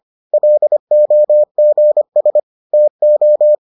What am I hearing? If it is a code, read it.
Morse code, heard as a keyed tone, about 25 wpm, 600 Hz. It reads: LOGS TO